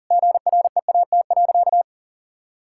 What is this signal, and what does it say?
Morse code, heard as a keyed tone, about 40 wpm, 700 Hz. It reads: GREAT.
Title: Sentences from Top 1000 Words - Review Part 6 - 40wpm